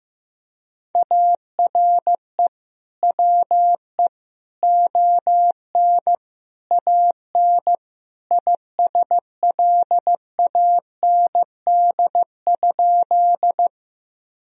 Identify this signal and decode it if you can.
Morse code, heard as a keyed tone, about 15 wpm, 700 Hz. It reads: ARE WE ON AN ISLAND?
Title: Sentences from Top 1000 Words - Review Part 3 - 15wpm